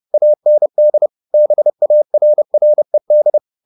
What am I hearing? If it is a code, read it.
Morse code, heard as a keyed tone, about 30 wpm, 600 Hz. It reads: AND BARRED